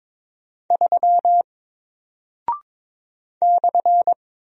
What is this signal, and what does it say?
Morse code, heard as a keyed tone, about 22 wpm, 700 Hz. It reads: 3 /
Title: Single Character (Letter:Number:Punctuation:Prosign) Alt 22wpm